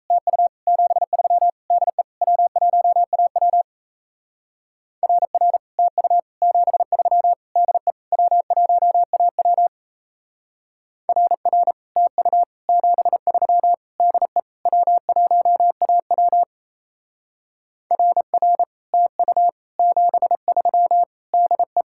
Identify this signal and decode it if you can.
Morse code, heard as a keyed tone, about 34 wpm, 700 Hz. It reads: TU 73 DE W1AW RR TU 73 DE W1AW RR TU 73 DE W1AW RR TU 73 DE
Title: Speed-Racing Common QSO Phrases - 28wpm